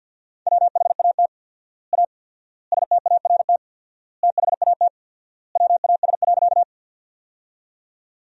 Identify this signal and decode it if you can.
Morse code, heard as a keyed tone, about 50 wpm, 700 Hz. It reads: WHAT A START THAT WAS.
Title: 4x Word-Spacing - Sentences from Top 300 Words - 50wpm